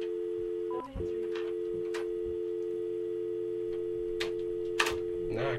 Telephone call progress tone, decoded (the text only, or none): dial tone